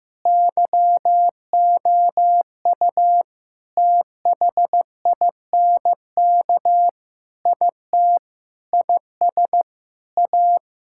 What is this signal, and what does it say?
Morse code, heard as a keyed tone, about 15 wpm, 700 Hz. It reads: YOU THINK IT IS A
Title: Sentences from Top 1000 Words - Review Part 6 - 15wpm